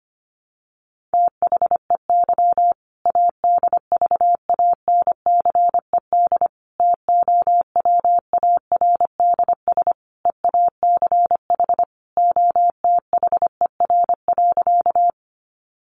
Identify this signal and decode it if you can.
Morse code, heard as a keyed tone, about 25 wpm, 700 Hz. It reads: THEY ADVANCED TOWARDS EACH OTHER.
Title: Sentences from Top 1200 Words 25wpm